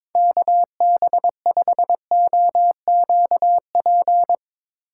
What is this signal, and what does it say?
Morse code, heard as a keyed tone, about 22 wpm, 700 Hz. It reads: KB5OQP
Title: USA Call Signs - 6 Characters - 4x Character Spacing - 22wpm